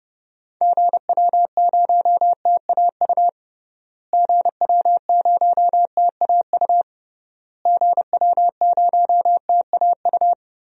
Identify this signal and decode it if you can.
Morse code, heard as a keyed tone, about 32 wpm, 700 Hz. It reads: GW0TAU GW0TAU GW0TAU
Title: DXCC-100 Call Signs - 6 Characters - Repeated 3x - 30wpm